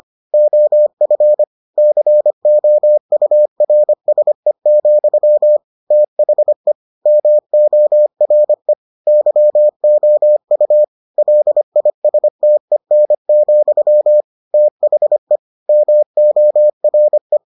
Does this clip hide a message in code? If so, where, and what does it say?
Yes: Morse code from 0.3 to 17.4 s, heard as a keyed tone, about 25 wpm, 600 Hz. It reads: OF COURSE, THE MORE YOU LISTEN, THE MORE